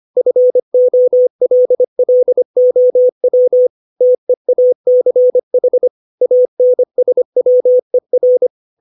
Morse code, heard as a keyed tone, about 25 wpm, 500 Hz. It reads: FOLLOW TEACH ANSWER
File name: 632 words / 632 Words 25WPM